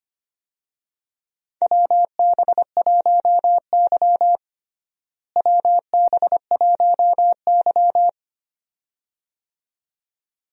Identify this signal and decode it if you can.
Morse code, heard as a keyed tone, about 25 wpm, 700 Hz. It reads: WB1Y WB1Y
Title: USA Call Signs - 4 Characters - Repeated 2x - 25wpm